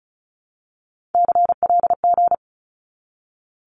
Morse code, heard as a keyed tone, about 36 wpm, 700 Hz. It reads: CLG
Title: Abbreviations 35wpm